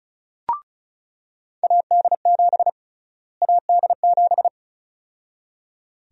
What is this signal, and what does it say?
Morse code, heard as a keyed tone, about 36 wpm, 700 Hz. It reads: AD7 AD7
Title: USA Call Sign Prefixes - Repeated 2x - 35wpm